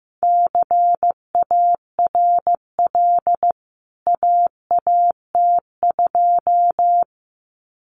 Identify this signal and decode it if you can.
Morse code, heard as a keyed tone, about 15 wpm, 700 Hz. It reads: CARL AAT2
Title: CWT Exchange 15wpm